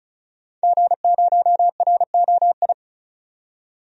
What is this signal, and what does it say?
Morse code, heard as a keyed tone, about 36 wpm, 700 Hz. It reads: G0ROI
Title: DXCC-100 Call Signs - 5 Characters - 35wpm